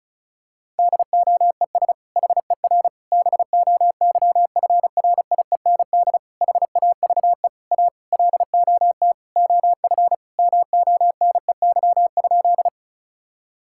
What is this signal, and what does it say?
Morse code, heard as a keyed tone, about 36 wpm, 700 Hz. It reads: DOES HER BOYFRIEND HAVE A LOT OF MONEY?
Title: Sentences from Top 1000 Words - Review Part 5 - 35wpm